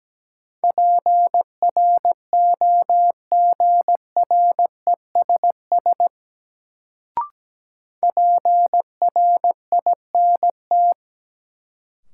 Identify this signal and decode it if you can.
Morse code, heard as a keyed tone, about 17 wpm, 700 Hz. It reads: PROGRESS PRINT
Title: Sentences from Top 1400 Words - New Word Drill Alt - 17wpm